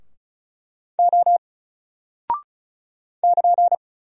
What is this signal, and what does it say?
Morse code, heard as a keyed tone, about 36 wpm, 700 Hz. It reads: O <KN>
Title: Single Character (Letter:Number:Punctuation:Prosign) Alt 35wpm